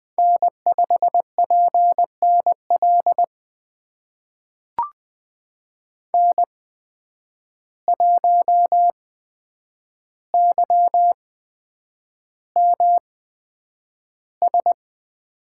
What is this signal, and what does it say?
Morse code, heard as a keyed tone, about 20 wpm, spaced as for 9 wpm, 700 Hz. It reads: N5PNL N1YMS
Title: USA Call Signs - 5 Characters - 8x Character Spacing - 20wpm